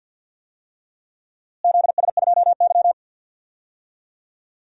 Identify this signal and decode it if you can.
Morse code, heard as a keyed tone, about 52 wpm, 700 Hz. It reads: ZS2Y
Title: DXCC-100 Call Signs - 4 Characters - Repeated 3x - 50wpm